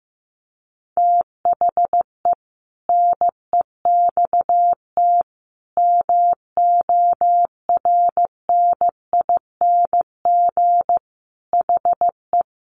Morse code, heard as a keyed tone, about 15 wpm, 700 Hz. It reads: THE NEXT MORNING HE
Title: Sentences from Top 600 Words 15wpm